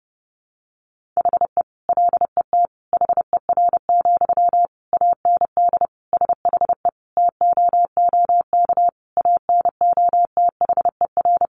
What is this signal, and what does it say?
Morse code, heard as a keyed tone, about 30 wpm, 700 Hz. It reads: HE LET HER, AND SHE TOOK ANOTHER
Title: Sentences from Top 1500 Words 30wpm